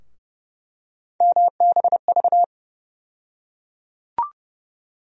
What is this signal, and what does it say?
Morse code, heard as a keyed tone, about 32 wpm, 700 Hz. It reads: MBV
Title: USA Call Sign Postfixes - 3 Letters 30wpm